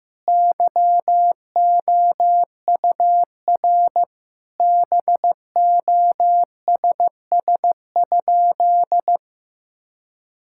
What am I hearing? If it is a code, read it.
Morse code, heard as a keyed tone, about 15 wpm, 700 Hz. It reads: YOUR BOSS?
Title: Sentences from Top 1000 Words - Review Part 6 - 15wpm